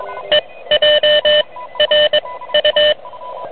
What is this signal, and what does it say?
Morse code, heard as a keyed tone, about 22 wpm, 3000 Hz, with noise. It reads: E J R U
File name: truSDX / MWC29.8